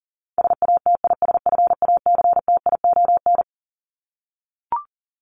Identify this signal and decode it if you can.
Morse code, heard as a keyed tone, about 40 wpm, 700 Hz. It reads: SATISFACTION
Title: Top 1751-2000 Words 40wpm